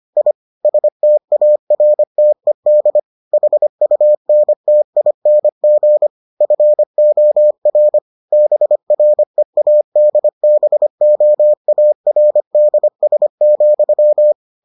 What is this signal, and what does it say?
Morse code, heard as a keyed tone, about 25 wpm, 600 Hz. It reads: I STARTED HUNTING FOR BREADBOARDS,